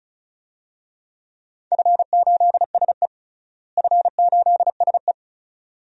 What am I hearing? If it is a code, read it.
Morse code, heard as a keyed tone, about 36 wpm, 700 Hz. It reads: F8SE F8SE
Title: DXCC-100 Call Signs - 4 Characters - Repeated 2x - 35wpm